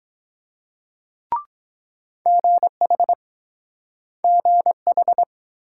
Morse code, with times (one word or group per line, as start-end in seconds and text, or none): 2.3-3.1 GH
4.2-5.2 GH